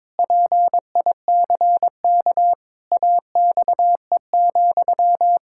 Morse code, heard as a keyed tone, about 22 wpm, 700 Hz. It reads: PICK AXE,
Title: Pangram Challenge v2 - 22wpm